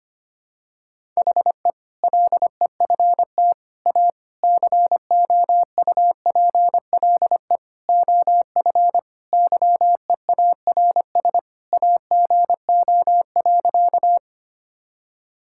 Morse code, heard as a keyed tone, about 25 wpm, 700 Hz. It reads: HE LEFT A COUPLE OF YEARS AGO.
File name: practice / 25wpm / Sentences with 4-Word Phrases - 25wpm